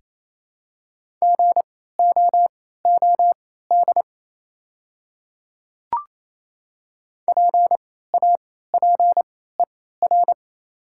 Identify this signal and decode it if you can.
Morse code, heard as a keyed tone, about 28 wpm, spaced as for 16 wpm, 700 Hz. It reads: GOOD PAPER
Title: 3x Character Spacing - Top 300 Words - 28wpm